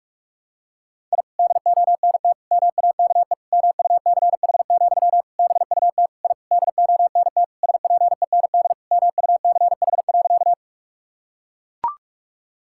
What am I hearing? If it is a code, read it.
Morse code, heard as a keyed tone, about 41 wpm, 700 Hz. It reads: I DONT MAKE MUCH, BUT I DONT SPEND MUCH.
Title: Sentences from Top 1000 Words - Review Part 5 - 45wpm